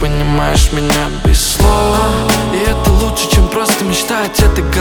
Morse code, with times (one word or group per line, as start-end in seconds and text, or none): none